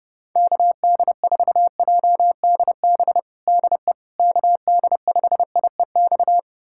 Morse code, heard as a keyed tone, about 30 wpm, 700 Hz. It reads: KD4JDB DE KD5IEX
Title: USA Callsign DE Callsign 30wpm